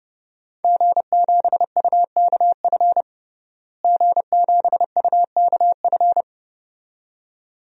Morse code, heard as a keyed tone, about 30 wpm, 700 Hz. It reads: G7UKF G7UKF
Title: DXCC-100 Call Signs - 5 Characters - Repeated 2x - 30wpm